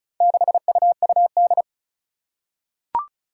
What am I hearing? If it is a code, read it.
Morse code, heard as a keyed tone, about 36 wpm, 700 Hz. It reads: 6UUD
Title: Four Challenging Letter-Number Combinations 35wpm